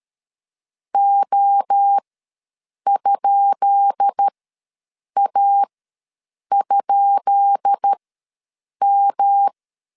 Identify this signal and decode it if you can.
Morse code, heard as a keyed tone, about 13 wpm, spaced as for 7 wpm, 800 Hz. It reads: O?A?M
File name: LEZIONI / 10 SIGNS / 10B INT ws 12wpm 3527